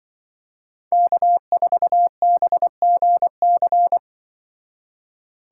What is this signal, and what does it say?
Morse code, heard as a keyed tone, about 24 wpm, 700 Hz. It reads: K4BGC
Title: Speed Racing - USA 1x3 Call Signs 20wpm